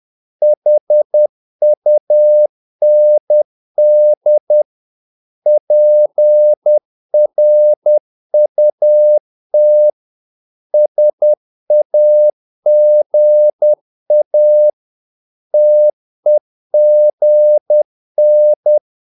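Morse code, heard as a keyed tone, about 10 wpm, 600 Hz. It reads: HUND PRUT SAGA TEGN
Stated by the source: Morse practice of short Danish words at 10 wpm, from CW med Gnister.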